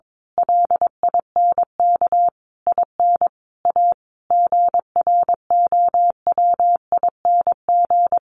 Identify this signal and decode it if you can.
Morse code, heard as a keyed tone, about 22 wpm, 700 Hz. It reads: LINK IN A GROWING
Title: A Sound Of Thunder 22wpm